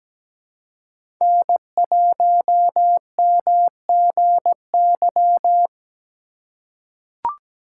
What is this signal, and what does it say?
Morse code, heard as a keyed tone, about 17 wpm, 700 Hz. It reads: N1MGY
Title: USA Call Signs - 4x Character Spacing - 17wpm